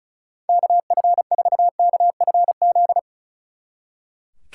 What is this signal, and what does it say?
Morse code, heard as a keyed tone, about 44 wpm, 700 Hz. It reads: KF4KFZ